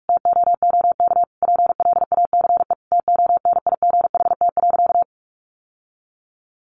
Morse code, heard as a keyed tone, about 46 wpm, 700 Hz. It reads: TOOK PLACE TONIGHT.
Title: Sentences from Top 1300 Words 45wpm